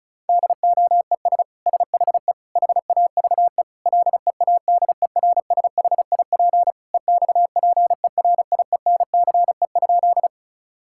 Morse code, heard as a keyed tone, about 36 wpm, 700 Hz. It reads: DOES SHE HAVE LEADERSHIP EXPERIENCE?
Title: Sentences from Top 1600 Words 35wpm